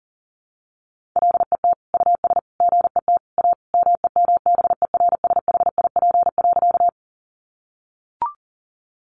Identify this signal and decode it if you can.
Morse code, heard as a keyed tone, about 40 wpm, 700 Hz. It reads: LET US GET A MEMBERSHIP.
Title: Sentences from Top 1400 Words 40wpm